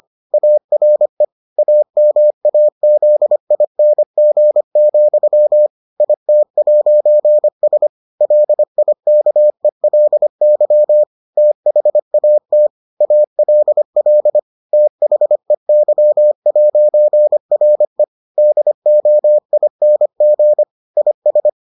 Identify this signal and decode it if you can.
Morse code, heard as a keyed tone, about 25 wpm, 600 Hz. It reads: ARE AMAZING, IT'S LIKELY THAT ALL THEY'RE DOING IS